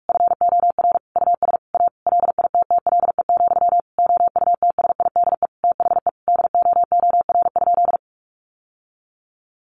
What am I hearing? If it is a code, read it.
Morse code, heard as a keyed tone, about 44 wpm, 700 Hz. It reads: FOR US A LITTLE, OUTSIDE THE DOOR?
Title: Sentences from Top 900 Words 45wpm